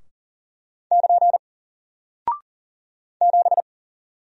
Morse code, heard as a keyed tone, about 40 wpm, 700 Hz. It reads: <KN> 7
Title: Single Character (Letter:Number:Punctuation:Prosign) Alt 40wpm